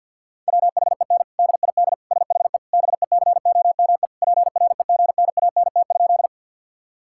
Morse code, heard as a keyed tone, about 52 wpm, 700 Hz. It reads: WHEN DID SHE BECOME PREGNANT?
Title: Sentences from Top 1000 Words - Review Part 6 - 50wpm